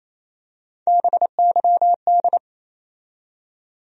Morse code, heard as a keyed tone, about 28 wpm, 700 Hz. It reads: BYD
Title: USA Call Sign Postfixes - 3 Letters 28wpm